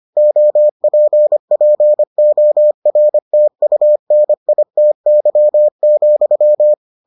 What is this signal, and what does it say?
Morse code, heard as a keyed tone, about 25 wpm, 600 Hz. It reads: OPPORTUNITY,